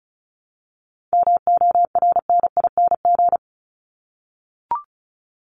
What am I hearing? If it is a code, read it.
Morse code, heard as a keyed tone, about 34 wpm, 700 Hz. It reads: MORNING
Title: Top 500 Words Repeated 35wpm